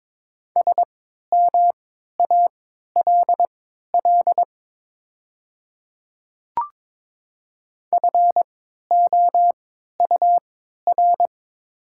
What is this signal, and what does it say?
Morse code, heard as a keyed tone, about 23 wpm, spaced as for 13 wpm, 700 Hz. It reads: SMALL FOUR
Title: 3x Character Spacing - Top 300 Words - 22wpm